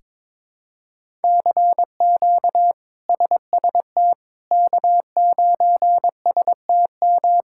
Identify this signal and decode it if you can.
Morse code, heard as a keyed tone, about 22 wpm, 700 Hz. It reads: CQ SST K9STM
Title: K1USN SST QSO Practice - 22wpm